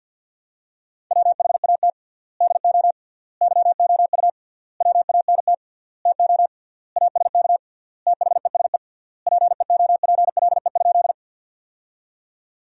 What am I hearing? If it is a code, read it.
Morse code, heard as a keyed tone, about 52 wpm, 700 Hz. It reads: WHAT DO YOU WANT TO ASK THESE PEOPLE?
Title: 3x Word-Spacing - Sentences from Top 200 Words - 50wpm